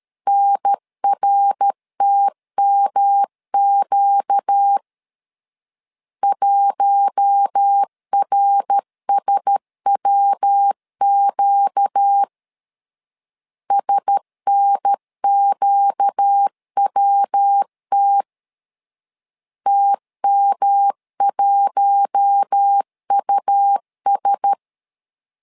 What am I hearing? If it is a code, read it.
Morse code, heard as a keyed tone, about 13 wpm, 800 Hz. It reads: NRTMQ 1RSWQ SNQWT TM1US
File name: LEZIONI / 4 Q1S 12wpm 1802